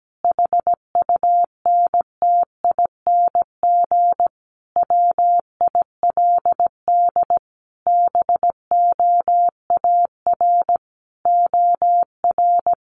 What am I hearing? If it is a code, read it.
Morse code, heard as a keyed tone, about 17 wpm, 700 Hz. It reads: HUNTING WILD BOAR OR
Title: A Sound Of Thunder 17wpm